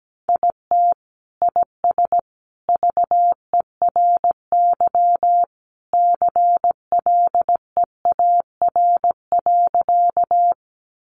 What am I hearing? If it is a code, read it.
Morse code, heard as a keyed tone, about 17 wpm, 700 Hz. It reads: IT IS VERY CLEAR.
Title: Sentences from Top 500 Words 17wpm